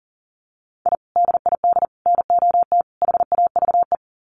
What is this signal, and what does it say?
Morse code, heard as a keyed tone, about 40 wpm, 700 Hz. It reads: I DID NOT HAVE
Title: Sets of 4 Words 40wpm